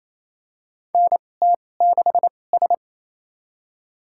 Morse code, heard as a keyed tone, about 28 wpm, spaced as for 20 wpm, 700 Hz. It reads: NT6S
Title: USA Call Signs - 4 Characters - 2x Character Spacing - 28wpm